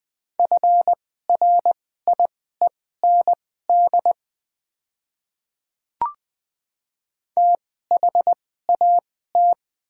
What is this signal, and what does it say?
Morse code, heard as a keyed tone, about 20 wpm, spaced as for 14 wpm, 700 Hz. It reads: FRIEND THAT
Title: 2x Character Spacing - Top 300 Words - 20wpm